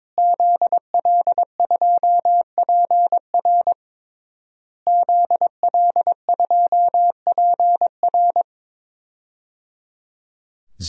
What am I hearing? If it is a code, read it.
Morse code, heard as a keyed tone, about 21 wpm, 700 Hz. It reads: ZL2PR ZL2PR
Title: DXCC-100 Call Signs - 5 Characters - Repeated 3x - 22wpm